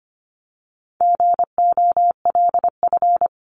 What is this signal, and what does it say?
Morse code, heard as a keyed tone, about 25 wpm, 700 Hz. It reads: GOLF
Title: Top 1751-2000 Words 25wpm